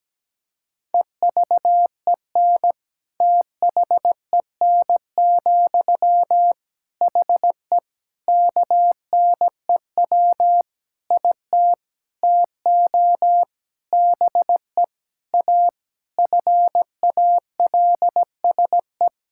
Morse code, heard as a keyed tone, about 17 wpm, 700 Hz. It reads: EVEN THEN, HE KNEW IT TO BE A FALSE
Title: Sentences from Top 1600 Words 17wpm